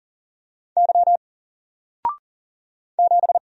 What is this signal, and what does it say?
Morse code, heard as a keyed tone, about 40 wpm, 700 Hz. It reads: Y 7
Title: Single Letter-Number 40wpm